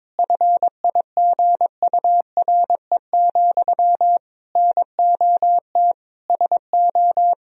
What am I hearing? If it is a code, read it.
Morse code, heard as a keyed tone, about 22 wpm, 700 Hz. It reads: FIGURE, NOT SO
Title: Sentences from Top 1000 Words - Review Part 2 - 22wpm